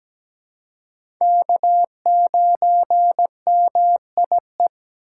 Morse code, heard as a keyed tone, about 17 wpm, 700 Hz. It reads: K9MIE
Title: USA Call Signs - 5 Characters - 6x Character Spacing - 17wpm